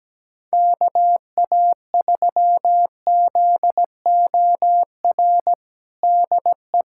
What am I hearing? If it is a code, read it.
Morse code, heard as a keyed tone, about 17 wpm, 700 Hz. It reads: KA3ZOR DE
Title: USA Callsign DE Callsign - 6 Characters - 17wpm